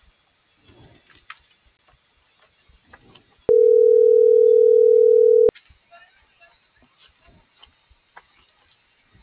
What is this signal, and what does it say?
Ringback tone (North America), 440 + 480 Hz.